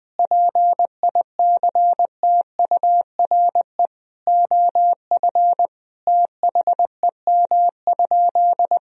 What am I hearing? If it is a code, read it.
Morse code, heard as a keyed tone, about 20 wpm, 700 Hz. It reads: PICTURE OF THEM?
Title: Sentences from Top 1000 Words - Review Part 2 - 20wpm